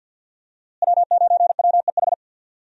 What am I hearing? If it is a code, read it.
Morse code, heard as a keyed tone, about 50 wpm, 700 Hz. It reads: W9WEH
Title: USA Call Signs - 5 Characters - Repeated 2x - 50wpm